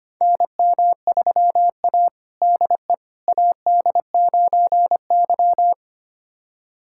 Morse code, heard as a keyed tone, about 25 wpm, 700 Hz. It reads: NM3A DE AD9Y
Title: USA Callsign DE Callsign - 4 Characters - 25wpm